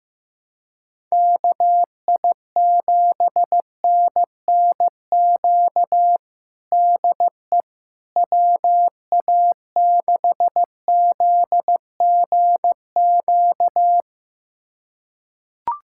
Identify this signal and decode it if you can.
Morse code, heard as a keyed tone, about 15 wpm, 700 Hz. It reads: KI7NNQ DE WA6ZGQ
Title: USA Callsign DE Callsign - 6 Characters - 15wpm